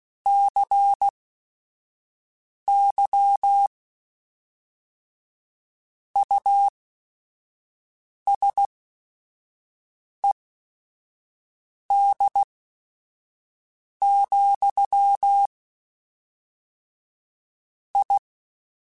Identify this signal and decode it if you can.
Morse code, heard as a keyed tone, about 16 wpm, spaced as for 5 wpm, 800 Hz. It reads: CY USED, I
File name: Technical Subjects / Technical Subjects-mp3s / Sw Pwr Sup @5